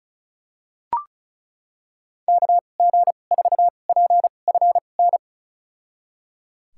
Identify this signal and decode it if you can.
Morse code, heard as a keyed tone, about 34 wpm, spaced as for 25 wpm, 700 Hz. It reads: KG4PFN
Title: USA Call Signs - 6 Characters - 2x Character Spacing - 35wpm